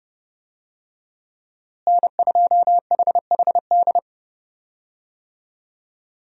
Morse code, heard as a keyed tone, about 30 wpm, 700 Hz. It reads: N2HHD
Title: USA Call Signs - 5 Characters - Repeated 2x - 30wpm